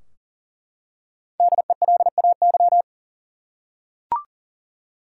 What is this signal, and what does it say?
Morse code, heard as a keyed tone, about 40 wpm, 700 Hz. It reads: DELAY
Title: Five Letter Words - 40wpm